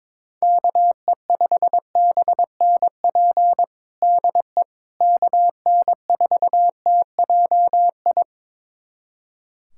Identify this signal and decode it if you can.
Morse code, heard as a keyed tone, about 21 wpm, 700 Hz. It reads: KE5BNP DE KN4TJI
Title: USA Callsign DE Callsign 22wpm